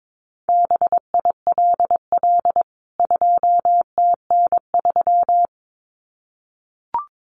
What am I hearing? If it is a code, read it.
Morse code, heard as a keyed tone, about 22 wpm, 700 Hz. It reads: BILL 2TN3
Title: CWT Exchange 22wpm